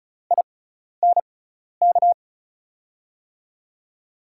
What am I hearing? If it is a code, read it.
Morse code, heard as a keyed tone, about 34 wpm, spaced as for 12 wpm, 700 Hz. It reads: INK
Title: USA Call Signs - 6 Characters - 6x Character Spacing - 35wpm